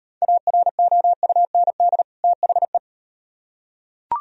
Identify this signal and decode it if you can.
Morse code, heard as a keyed tone, about 38 wpm, 700 Hz. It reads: AROUND THE